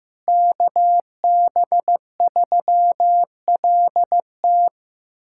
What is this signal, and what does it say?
Morse code, heard as a keyed tone, about 15 wpm, 700 Hz. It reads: KB3LT
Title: USA Callsign DE Callsign 15wpm